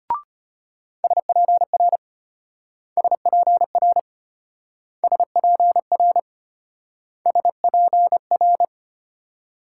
Morse code, heard as a keyed tone, about 32 wpm, 700 Hz. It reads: SPR SPR SPR SPR